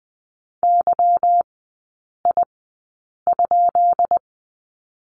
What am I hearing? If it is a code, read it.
Morse code, heard as a keyed tone, about 20 wpm, spaced as for 8 wpm, 700 Hz. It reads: YI?
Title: 145 Lesson 24e - Letter y - Three Characters 20wpm